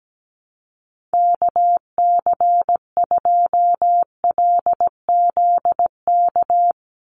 Morse code, heard as a keyed tone, about 17 wpm, 700 Hz. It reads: KC2LZK
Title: USA Call Signs 17wpm